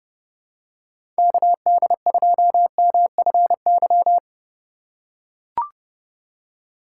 Morse code, heard as a keyed tone, about 30 wpm, 700 Hz. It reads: KD2MFY
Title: USA Call Signs - 6 Characters - 3x Character Spacing - 30wpm